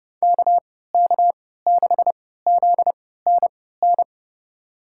Morse code, heard as a keyed tone, about 31 wpm, spaced as for 17 wpm, 700 Hz. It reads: KK6ZNN
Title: USA Call Signs - 3x Character Spacing - 30wpm